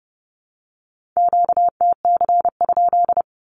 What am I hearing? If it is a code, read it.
Morse code, heard as a keyed tone, about 30 wpm, 700 Hz. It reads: QTC?
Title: Q-Codes with Spoken Meaning - 30wpm